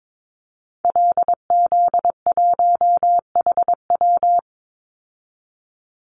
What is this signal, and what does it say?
Morse code, heard as a keyed tone, about 22 wpm, 700 Hz. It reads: LZ1HW
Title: Call Signs Repeated 3x - 22wpm